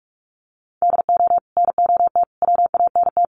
Morse code, heard as a keyed tone, about 44 wpm, 700 Hz. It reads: DO NOT WANT
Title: Sets of 5 Words 45wpm